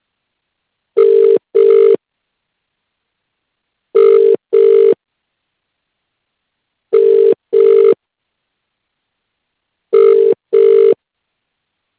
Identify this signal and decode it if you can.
Ringback tone (UK), 400 + 450 Hz.